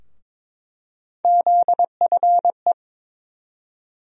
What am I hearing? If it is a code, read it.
Morse code, heard as a keyed tone, about 22 wpm, 700 Hz. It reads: ZFE
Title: USA Call Sign Postfixes - 3 Letters 22wpm